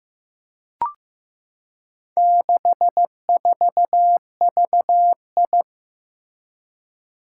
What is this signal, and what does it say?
Morse code, heard as a keyed tone, about 15 wpm, 700 Hz. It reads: 64VI
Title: Four Challenging Letter-Number Combinations 15wpm